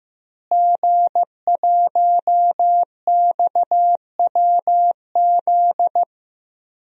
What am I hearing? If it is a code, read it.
Morse code, heard as a keyed tone, about 15 wpm, 700 Hz. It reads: G1XWZ
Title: DXCC-100 Call Signs - 5 Characters - Repeated 3x - 15wpm